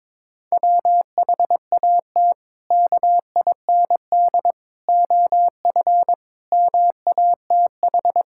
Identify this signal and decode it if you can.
Morse code, heard as a keyed tone, about 22 wpm, 700 Hz. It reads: WHAT KIND OF MATH
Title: Sentences from Top 1000 Words - Review Part 6 - 22wpm